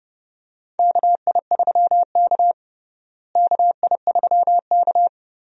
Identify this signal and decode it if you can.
Morse code, heard as a keyed tone, about 31 wpm, 700 Hz. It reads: KI3K KI3K
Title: USA Call Signs - 4 Characters - Repeated 3x - 30wpm